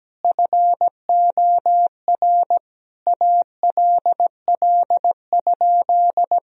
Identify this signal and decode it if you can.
Morse code, heard as a keyed tone, about 17 wpm, 700 Hz. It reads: FOR ALL?
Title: Sentences from Top 1000 Words - Review Part 3 - 17wpm